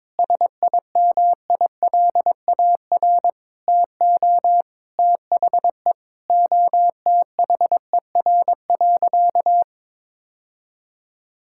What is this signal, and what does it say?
Morse code, heard as a keyed tone, about 23 wpm, 700 Hz. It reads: SIMILAR TO THE OTHER.
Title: Sentences from Top 1000 Words - Review Part 5 - 22wpm